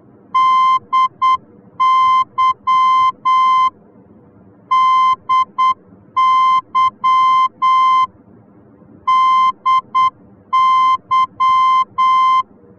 Morse code, heard as a keyed tone, about 8 wpm, 1040 Hz. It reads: DY DY DY
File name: germany / mx / DY-D